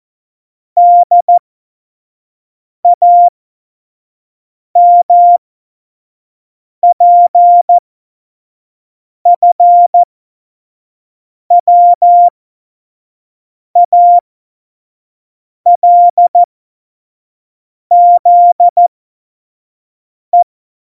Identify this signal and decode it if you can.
Morse code, heard as a keyed tone, about 14 wpm, spaced as for 5 wpm, 700 Hz. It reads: DAMPFWALZE
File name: Morsekurs / Witze / Witz 13 March 2018, 25 F70